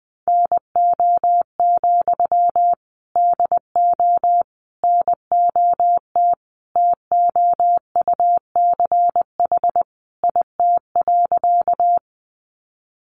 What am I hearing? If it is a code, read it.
Morse code, heard as a keyed tone, about 20 wpm, 700 Hz. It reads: NO, DO NOT TOUCH IT.
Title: Sentences from Top 1200 Words 20wpm